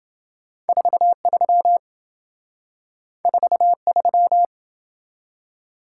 Morse code, heard as a keyed tone, about 27 wpm, 700 Hz. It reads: 43 43